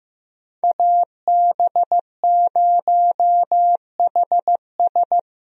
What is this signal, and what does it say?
Morse code, heard as a keyed tone, about 15 wpm, 700 Hz. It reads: AB0HS
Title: USA Call Signs - 5 Characters - Repeated 2x - 15wpm